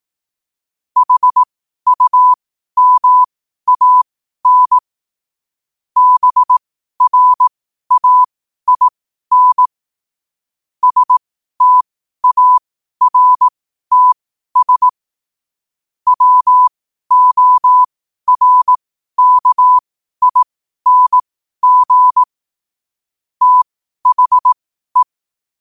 Morse code, heard as a keyed tone, about 18 wpm, spaced as for 12 wpm, 1000 Hz. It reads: HUMAN BRAIN STARTS WORKING THE
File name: qotdfiles / 12wpm